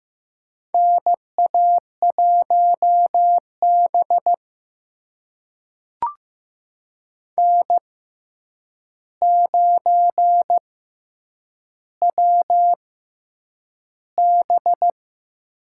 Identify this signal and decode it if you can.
Morse code, heard as a keyed tone, about 15 wpm, spaced as for 8 wpm, 700 Hz. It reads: NA1B N9WB
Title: USA Call Signs - 4 Characters - 6x Character Spacing - 15wpm